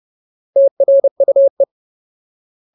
Morse code, heard as a keyed tone, about 30 wpm, 550 Hz. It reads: TRUE
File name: Ultimate Word List / 30WPM Ultimate